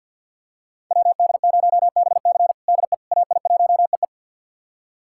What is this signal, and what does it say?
Morse code, heard as a keyed tone, about 53 wpm, 700 Hz. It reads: WD0BC DE AI1EE
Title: USA Callsign DE Callsign - 5 Characters - 50wpm